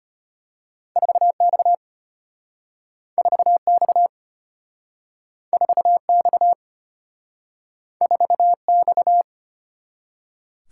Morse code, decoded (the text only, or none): 4X 4X 4X 4X